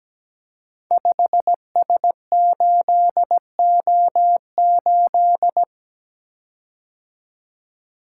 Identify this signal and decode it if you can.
Morse code, heard as a keyed tone, about 17 wpm, 700 Hz. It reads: 5S8O8
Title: Code Groups - 5 Letter:Numbers - 17wpm